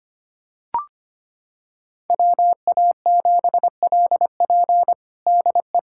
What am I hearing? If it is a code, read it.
Morse code, heard as a keyed tone, about 25 wpm, 700 Hz. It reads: WA7LP DE
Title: USA Callsign DE Callsign - 5 Characters - 25wpm